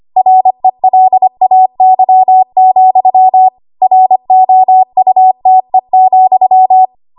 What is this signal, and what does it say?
Morse code, heard as a keyed tone, about 25 wpm, 750 Hz. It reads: RELAY, ROUTE,
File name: Morse / 25WPM